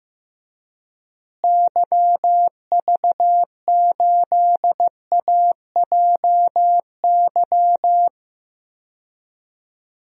Morse code, heard as a keyed tone, about 15 wpm, 700 Hz. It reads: YV8AJY
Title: DXCC-100 Call Signs - 6 Characters - Repeated 3x - 15wpm